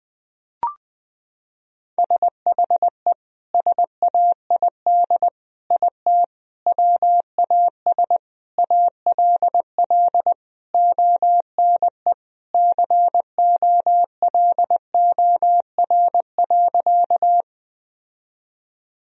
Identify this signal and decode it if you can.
Morse code, heard as a keyed tone, about 20 wpm, 700 Hz. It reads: SHE SAID IT WAS ALL ONE COLOR.
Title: Sentences from Top 1000 Words - Review Part 2 - 20wpm